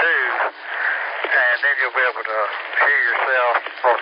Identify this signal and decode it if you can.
Emergency Alert System (SAME) tones, heard as harsh buzzing bursts, with speech. It is heard from 0.6 to 3.4 s.